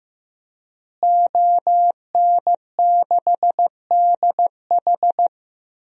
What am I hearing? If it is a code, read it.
Morse code, heard as a keyed tone, about 15 wpm, 700 Hz. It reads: ON6DH
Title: DXCC-100 Call Signs - 5 Characters - Repeated 3x - 15wpm